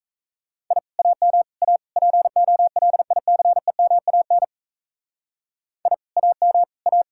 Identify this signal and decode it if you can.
Morse code, heard as a keyed tone, about 33 wpm, 700 Hz. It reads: I AM A POLICEMAN I AM A
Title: Speed-Racing Common QSO Phrases Encore - 28wpm